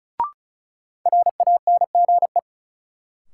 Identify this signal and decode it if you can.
Morse code, heard as a keyed tone, about 36 wpm, 700 Hz. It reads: RANGE
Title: Sets of 4 Words - New Word Drill - 35wpm